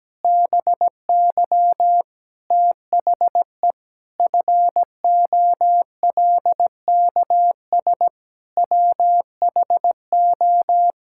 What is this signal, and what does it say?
Morse code, heard as a keyed tone, about 17 wpm, 700 Hz. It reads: BY THE FOLKS WHO
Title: Pangram Challenge v2 - 17wpm